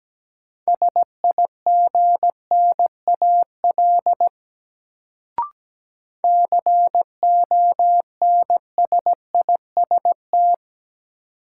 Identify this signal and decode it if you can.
Morse code, heard as a keyed tone, about 17 wpm, 700 Hz. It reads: SIGNAL CONSIST
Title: Sentences from Top 1400 Words - New Word Drill - 17wpm